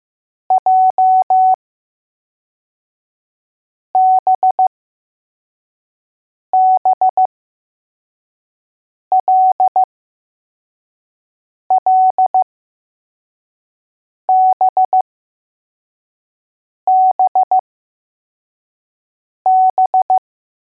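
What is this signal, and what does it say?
Morse code, heard as a keyed tone, about 15 wpm, spaced as for 4 wpm, 750 Hz. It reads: JBBLLBBB
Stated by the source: Morse code letters test drill for L, J and B.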